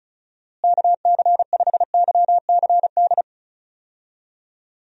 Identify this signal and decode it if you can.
Morse code, heard as a keyed tone, about 36 wpm, 700 Hz. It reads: KC5YCD
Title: USA Call Signs - 6 Characters - Repeated 2x - 35wpm